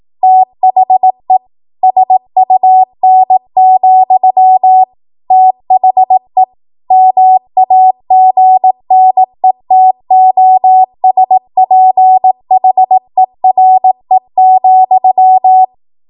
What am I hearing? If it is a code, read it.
Morse code, heard as a keyed tone, about 18 wpm, 750 Hz. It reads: THE SUN, THE MAGNETOSPHERE,
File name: Morse / 18WPM